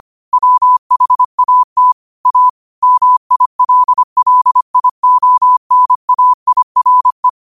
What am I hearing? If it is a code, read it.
Morse code, heard as a keyed tone, about 25 wpm, 1000 Hz. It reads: WHAT A MILLIONAIRE
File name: qotdfiles / 25wpm